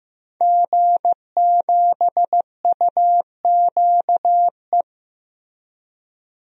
Morse code, heard as a keyed tone, about 15 wpm, 700 Hz. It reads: G7UQE
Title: DXCC-100 Call Signs - 5 Characters - Repeated 2x - 15wpm